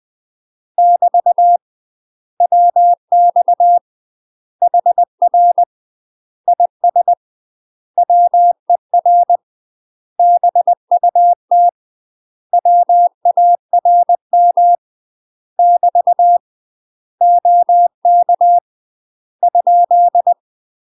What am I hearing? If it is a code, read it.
Morse code, heard as a keyed tone, about 20 wpm, 700 Hz. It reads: <BT> WX HR IS WER BUT WARM <BT> OK ?
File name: UK MASTER QSOs TRAINING / 20WPM / UK MASTER QSOs 0051